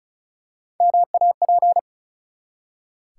Morse code, heard as a keyed tone, about 36 wpm, 700 Hz. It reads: MAP
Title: Three Letter Words - 35wpm